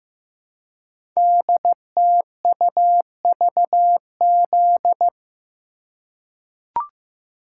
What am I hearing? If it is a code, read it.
Morse code, heard as a keyed tone, about 15 wpm, 700 Hz. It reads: DTUVZ
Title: Code Groups - 5 Letters - 15wpm